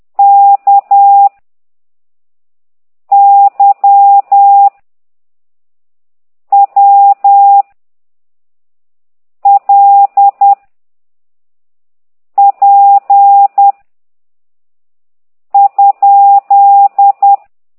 Morse code, heard as a keyed tone, about 10 wpm, 800 Hz. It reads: K Y W L P ?